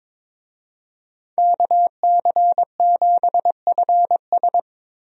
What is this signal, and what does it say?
Morse code, heard as a keyed tone, about 22 wpm, 700 Hz. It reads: KC7FS
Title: USA Call Signs - 5 Characters - Repeated 3x - 22wpm